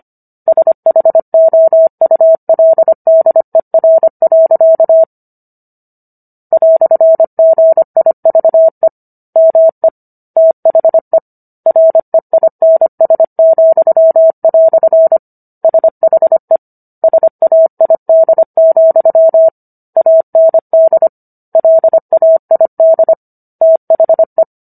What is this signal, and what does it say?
Morse code, heard as a keyed tone, about 25 wpm, 650 Hz. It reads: SHOULDER. "GIVE ME THE REINS," SHE SAID, AND LAID THE